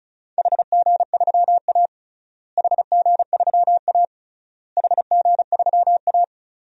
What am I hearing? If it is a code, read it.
Morse code, heard as a keyed tone, about 34 wpm, 700 Hz. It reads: HG3A HG3A HG3A
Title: DXCC-100 Call Signs - 4 Characters - Repeated 3x - 35wpm